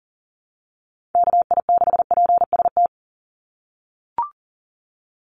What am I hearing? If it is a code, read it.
Morse code, heard as a keyed tone, about 40 wpm, 700 Hz. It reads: KI6PST
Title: USA Call Signs 40wpm